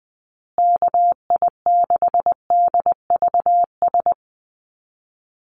Morse code, heard as a keyed tone, about 20 wpm, 700 Hz. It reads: KI6DVS
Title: USA Call Signs Repeated 2x 20wpm